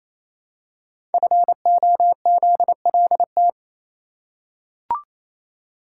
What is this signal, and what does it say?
Morse code, heard as a keyed tone, about 29 wpm, 700 Hz. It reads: FOZLT
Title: Code Groups - 5 Letters - 28wpm